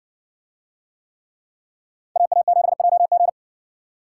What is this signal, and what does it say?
Morse code, heard as a keyed tone, about 60 wpm, 700 Hz. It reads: AA7JG